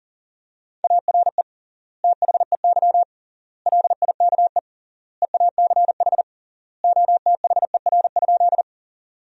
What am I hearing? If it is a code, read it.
Morse code, heard as a keyed tone, about 40 wpm, 700 Hz. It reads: ARE THEY LIKE EACH OTHER?
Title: 3x Word-Spacing - Sentences from Top 200 Words - 40wpm